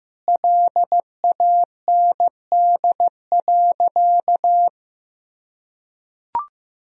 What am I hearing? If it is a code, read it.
Morse code, heard as a keyed tone, about 15 wpm, 700 Hz. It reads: LAND.
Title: Sentences from Top 1000 Words - Review Part 1 - 15wpm